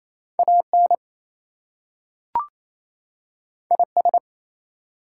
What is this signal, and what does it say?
Morse code, heard as a keyed tone, about 28 wpm, 700 Hz. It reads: AN IS
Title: Two Letter Words - 28wpm